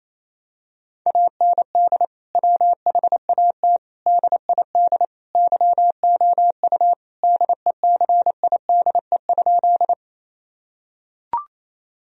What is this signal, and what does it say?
Morse code, heard as a keyed tone, about 28 wpm, 700 Hz. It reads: AND WHAT DID YOU DECIDE?
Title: Sentences from Top 1000 Words - Review Part 3 - 28wpm